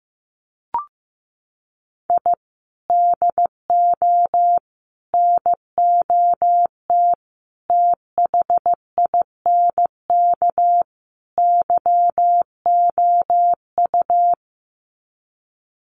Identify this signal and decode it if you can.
Morse code, heard as a keyed tone, about 15 wpm, 700 Hz. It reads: I DO NOT THINK YOU
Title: Sets of 5 Words 15wpm